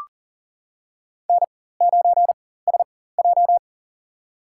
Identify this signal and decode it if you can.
Morse code, heard as a keyed tone, about 37 wpm, spaced as for 18 wpm, 700 Hz. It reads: N9SJ
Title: USA Call Signs - 4 Characters - 4x Character Spacing - 40wpm